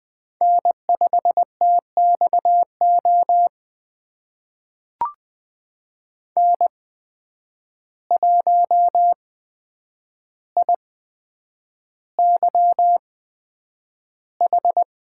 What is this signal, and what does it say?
Morse code, heard as a keyed tone, about 20 wpm, spaced as for 9 wpm, 700 Hz. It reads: N5TXO N1IYH
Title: USA Call Signs - 5 Characters - 8x Character Spacing - 20wpm